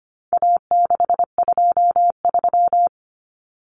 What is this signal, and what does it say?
Morse code, heard as a keyed tone, about 25 wpm, 700 Hz. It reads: A623
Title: Common Cut Numbers - 4-Digit 25wpm